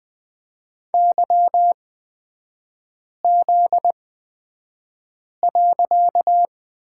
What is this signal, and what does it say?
Morse code, heard as a keyed tone, about 20 wpm, spaced as for 5 wpm, 690 Hz. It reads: YZ.
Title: 255 Lesson 41e - Period - Three Characters 20wpm